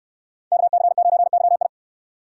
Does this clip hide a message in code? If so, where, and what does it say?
Yes: Morse code from 0.5 to 1.7 s, heard as a keyed tone, about 65 wpm, 700 Hz. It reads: KC9YI